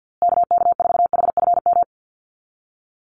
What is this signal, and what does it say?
Morse code, heard as a keyed tone, about 50 wpm, 700 Hz. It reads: KK4HFM
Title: USA Call Signs Repeated 3x 50wpm